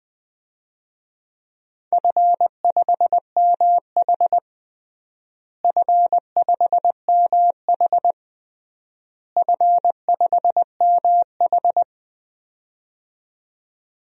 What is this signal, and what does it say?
Morse code, heard as a keyed tone, about 21 wpm, 700 Hz. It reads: F5MH F5MH F5MH
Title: DXCC-100 Call Signs - 4 Characters - Repeated 3x - 20wpm